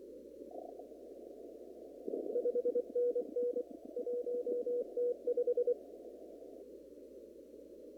Morse code, heard as a keyed tone, about 25 wpm, 490 Hz. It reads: 5NN 1T5
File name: CW / kt5j wpx cw-op-k5pi